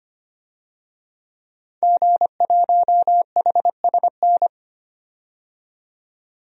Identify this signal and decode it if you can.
Morse code, heard as a keyed tone, about 25 wpm, 700 Hz. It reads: G1HSN
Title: DXCC-100 Call Signs - 5 Characters - 25wpm